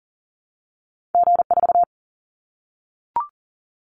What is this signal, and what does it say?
Morse code, heard as a keyed tone, about 40 wpm, 700 Hz. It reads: G4
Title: Callsign Prefixes for Top 25 DXCC - Repeated 3x - 40wpm